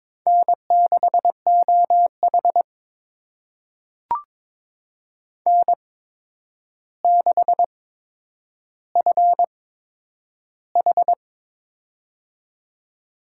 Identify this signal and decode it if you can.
Morse code, heard as a keyed tone, about 22 wpm, spaced as for 9 wpm, 700 Hz. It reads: N6OH N6FH
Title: USA Call Signs - 8x Character Spacing - 22wpm